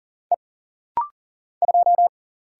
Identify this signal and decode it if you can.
Morse code, heard as a keyed tone, about 42 wpm, 700 Hz. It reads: E 2
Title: Single Letter-Number - Rapid-Fire 40wpm